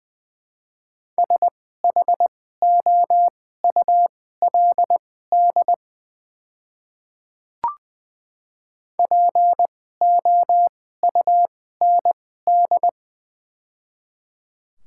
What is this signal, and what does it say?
Morse code, heard as a keyed tone, about 20 wpm, spaced as for 15 wpm, 700 Hz. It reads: SHOULD POUND
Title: 2x Character Spacing - Top 500 Words - 20wpm